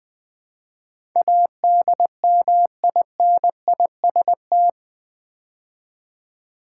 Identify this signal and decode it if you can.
Morse code, heard as a keyed tone, about 20 wpm, 700 Hz. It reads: ADMINIST
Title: Build A Word - 4 Letters At A Time - 10-Letter Words - 20wpm